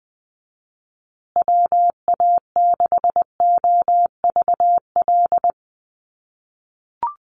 Morse code, heard as a keyed tone, about 20 wpm, 700 Hz. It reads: WA6OVL
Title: USA Call Signs Repeated 2x 20wpm